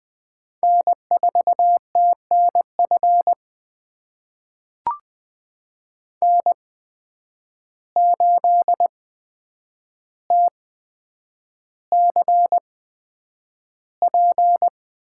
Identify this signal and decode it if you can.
Morse code, heard as a keyed tone, about 20 wpm, spaced as for 9 wpm, 700 Hz. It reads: N4TNF N8TCP
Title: USA Call Signs - 5 Characters - 8x Character Spacing - 20wpm